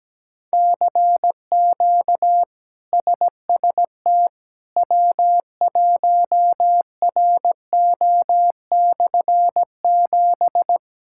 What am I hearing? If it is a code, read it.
Morse code, heard as a keyed tone, about 17 wpm, 700 Hz. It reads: CQ SST W1RO/7
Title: K1USN SST QSO Practice - 17wpm